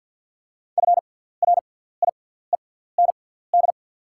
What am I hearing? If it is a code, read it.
Morse code, heard as a keyed tone, about 48 wpm, spaced as for 17 wpm, 690 Hz. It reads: FRIEND
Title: 6x Character Spacing - Top 500 Words - 50wpm